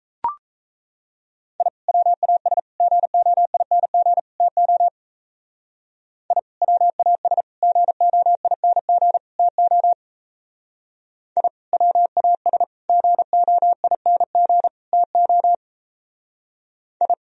Morse code, heard as a keyed tone, about 36 wpm, 700 Hz. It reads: I WAS GOING TO I WAS GOING TO I WAS GOING TO I